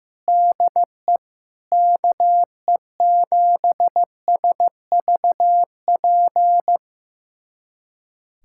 Morse code, heard as a keyed tone, about 15 wpm, 700 Hz. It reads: DE KE7SVP
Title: USA Callsign DE Callsign - 6 Characters - 15wpm